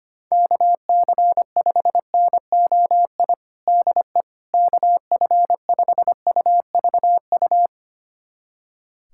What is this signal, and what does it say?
Morse code, heard as a keyed tone, about 25 wpm, 700 Hz. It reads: KC5NOI DE KF5UVU
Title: USA Callsign DE Callsign - 6 Characters - 25wpm